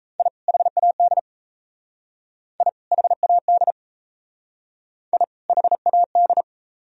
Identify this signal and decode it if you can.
Morse code, heard as a keyed tone, about 33 wpm, 700 Hz. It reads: I HAD I HAD I HAD
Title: Speed-Racing - Sets of 2 Words 28wpm